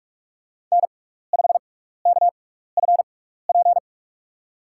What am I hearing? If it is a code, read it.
Morse code, heard as a keyed tone, about 45 wpm, spaced as for 15 wpm, 700 Hz. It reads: N5KFP
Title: USA Call Signs - 6x Character Spacing - 45wpm